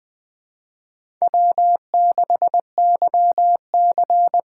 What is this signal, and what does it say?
Morse code, heard as a keyed tone, about 20 wpm, 700 Hz. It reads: W6YC
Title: Speed Racing - USA 1x2 Call Signs 17wpm